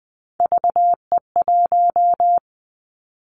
Morse code, heard as a keyed tone, about 20 wpm, 700 Hz. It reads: VE1
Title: Callsign Prefixes for Top 100 DXCC - 20wpm